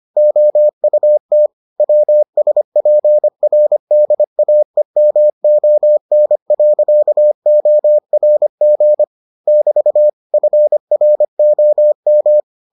Morse code, heard as a keyed tone, about 25 wpm, 600 Hz. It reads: OUT WSPRDAEMON.ORG <BT> FROM